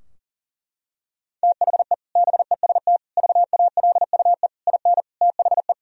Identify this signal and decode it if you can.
Morse code, heard as a keyed tone, about 40 wpm, 700 Hz. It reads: THE BEST VALUE IN THE
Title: Sentences from Top 1000 Words - Review Part 3 - 40wpm